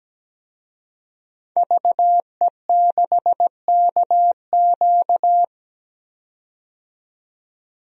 Morse code, heard as a keyed tone, about 17 wpm, 700 Hz. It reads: VE6KQ
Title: DXCC-100 Call Signs - 5 Characters - 17wpm